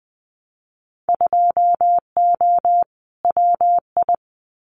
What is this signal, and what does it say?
Morse code, heard as a keyed tone, about 20 wpm, 700 Hz. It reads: 2O WI
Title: Winter Field Day Exchanges 20wpm